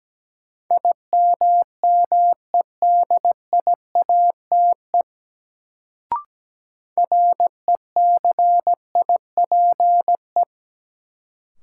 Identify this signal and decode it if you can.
Morse code, heard as a keyed tone, about 17 wpm, 700 Hz. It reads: IMMEDIATE RECIPE
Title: Sentences from Top 1200 Words - New Word Drill - 17wpm